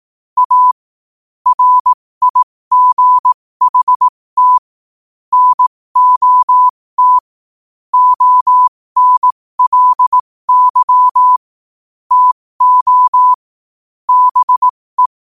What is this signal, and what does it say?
Morse code, heard as a keyed tone, about 18 wpm, spaced as for 15 wpm, 1000 Hz. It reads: A RIGHT NOT ONLY TO BE
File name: qotdfiles / 15wpm